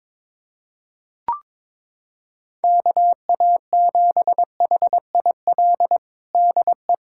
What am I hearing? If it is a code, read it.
Morse code, heard as a keyed tone, about 22 wpm, 700 Hz. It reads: KA7HIL DE
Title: USA Callsign DE Callsign - 6 Characters - 22wpm